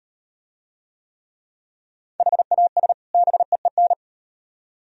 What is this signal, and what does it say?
Morse code, heard as a keyed tone, about 34 wpm, 700 Hz. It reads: HAS BEEN